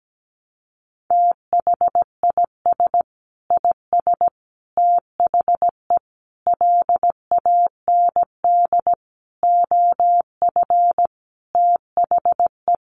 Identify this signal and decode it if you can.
Morse code, heard as a keyed tone, about 17 wpm, 700 Hz. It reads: THIS IS THE LAND OF THE
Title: Sentences from Top 1300 Words 17wpm